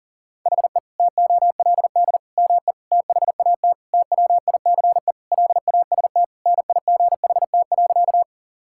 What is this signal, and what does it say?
Morse code, heard as a keyed tone, about 40 wpm, 700 Hz. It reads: HE TOLD ME THAT TWICE LAST NIGHT.
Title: Sentences from Top 1600 Words 40wpm